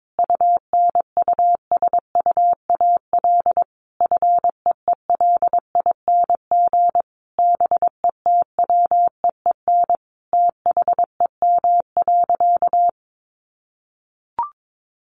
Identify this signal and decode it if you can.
Morse code, heard as a keyed tone, about 22 wpm, 700 Hz. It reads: UNUSUAL FEELING BETWEEN THEM.
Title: Sentences from Top 1400 Words 22wpm